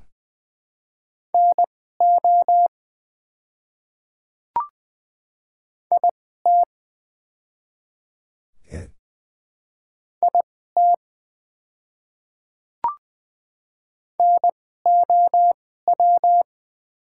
Morse code, heard as a keyed tone, about 20 wpm, spaced as for 15 wpm, 700 Hz. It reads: NO IT IT NOW
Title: 2x Character Spacing - Top 200 Words - 20wpm